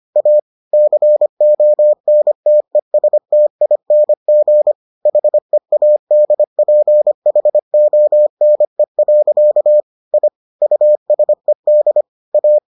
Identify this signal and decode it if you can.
Morse code, heard as a keyed tone, about 25 wpm, 600 Hz. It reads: A CONTESTING HEADPHONE. I USED A